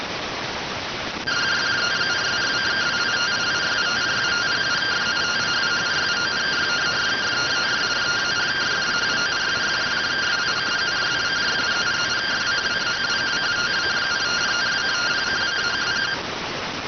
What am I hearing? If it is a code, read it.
Two-tone teleprinter signal, about 170 Hz shift.